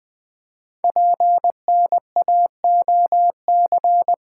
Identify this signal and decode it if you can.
Morse code, heard as a keyed tone, about 20 wpm, 700 Hz. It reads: PNAOC
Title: Code Groups - 5 Letters - 20wpm